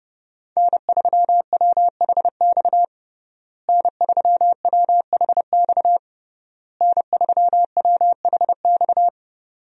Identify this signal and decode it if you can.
Morse code, heard as a keyed tone, about 30 wpm, 700 Hz. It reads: N3WHX N3WHX N3WHX
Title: USA Call Signs - 5 Characters - Repeated 3x - 30wpm